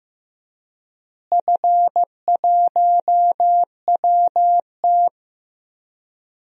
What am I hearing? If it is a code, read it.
Morse code, heard as a keyed tone, about 15 wpm, 700 Hz. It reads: F1WT
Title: DXCC-100 Call Signs - 4 Characters - Repeated 3x - 15wpm